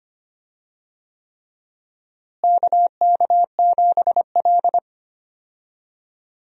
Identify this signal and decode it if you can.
Morse code, heard as a keyed tone, about 25 wpm, 700 Hz. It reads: KK7L
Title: USA Call Signs - 4 Characters - Repeated 3x - 25wpm